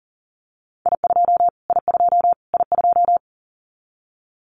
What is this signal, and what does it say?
Morse code, heard as a keyed tone, about 40 wpm, 700 Hz. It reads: I2 I2 I2
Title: Callsign Prefixes for Top 25 DXCC - Repeated 3x - 40wpm